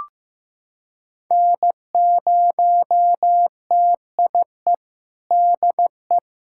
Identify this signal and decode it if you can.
Morse code, heard as a keyed tone, about 15 wpm, 700 Hz. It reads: N0TIE DE
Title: USA Callsign DE Callsign - 5 Characters - 15wpm